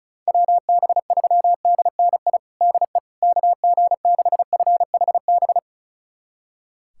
Morse code, heard as a keyed tone, about 34 wpm, 700 Hz. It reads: WB3DNI DE KG6FHB
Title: USA Callsign DE Callsign - 6 Characters - 35wpm